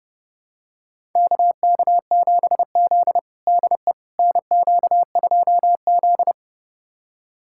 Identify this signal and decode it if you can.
Morse code, heard as a keyed tone, about 30 wpm, 700 Hz. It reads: KK7Z DE NQ2Z
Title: USA Callsign DE Callsign - 4 Characters - 30wpm